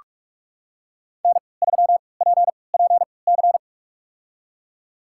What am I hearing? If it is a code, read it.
Morse code, heard as a keyed tone, about 46 wpm, spaced as for 25 wpm, 700 Hz. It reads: N3PPC
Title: USA Call Signs - 5 Characters - 3x Character Spacing - 45wpm